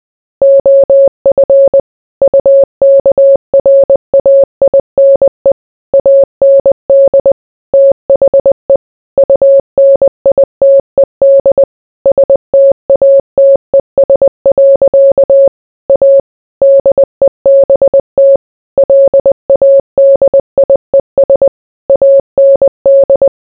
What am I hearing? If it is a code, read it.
Morse code, heard as a keyed tone, about 20 wpm, 550 Hz. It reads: OF UKRAINE AND THE UNITED STATES. A DEBT LADIES AND